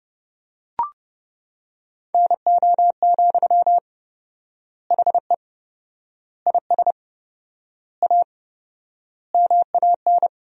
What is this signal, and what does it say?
Morse code, heard as a keyed tone, about 30 wpm, 700 Hz. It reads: NO, HE IS A MAN
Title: 4x Word-Spacing - Sentences from Top 400 Words - 30wpm